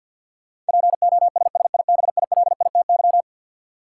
Morse code, heard as a keyed tone, about 52 wpm, 700 Hz. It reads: POSSIBILITY
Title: Sets of 3 Words - New Word Drill - 50wpm